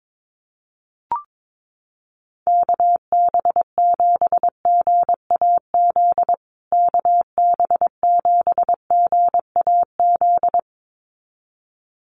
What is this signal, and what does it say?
Morse code, heard as a keyed tone, about 22 wpm, 700 Hz. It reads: KB7GAZ KB7GAZ
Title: USA Call Signs Repeated 2x 22wpm